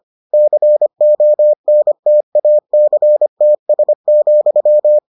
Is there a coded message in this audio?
Yes: Morse code from 0.3 to 5.0 s, heard as a keyed tone, about 25 wpm, 600 Hz. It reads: CONTACTS,